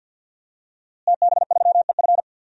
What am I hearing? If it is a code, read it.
Morse code, heard as a keyed tone, about 52 wpm, 700 Hz. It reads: TB3EF
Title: DXCC-100 Call Signs - 5 Characters - Repeated 3x - 50wpm